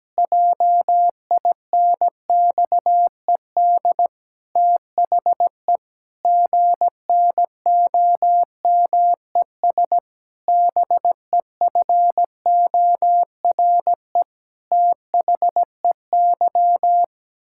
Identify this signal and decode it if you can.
Morse code, heard as a keyed tone, about 17 wpm, 700 Hz. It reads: JINXED THE GNOMES BEFORE THEY
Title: Pangram Challenge v2 - 17wpm